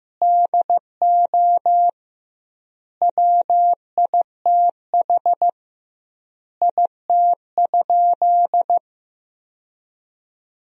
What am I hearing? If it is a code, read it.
Morse code, heard as a keyed tone, about 15 wpm, 700 Hz. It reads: DO WITH IT?
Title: 2x Word-Spacing - Sentences from Top 100 Words - 15wpm